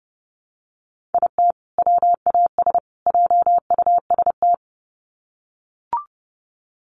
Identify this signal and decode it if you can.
Morse code, heard as a keyed tone, about 30 wpm, 700 Hz. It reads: IT WAS JUST
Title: Sets of 3 Words 30wpm